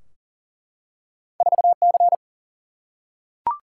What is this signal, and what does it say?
Morse code, heard as a keyed tone, about 42 wpm, 700 Hz. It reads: 4C
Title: Single-Letter Single-Number 40wpm